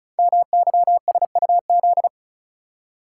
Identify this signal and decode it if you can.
Morse code, heard as a keyed tone, about 36 wpm, 700 Hz. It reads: MYSUZ
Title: Code Groups - 5 Letters - 35wpm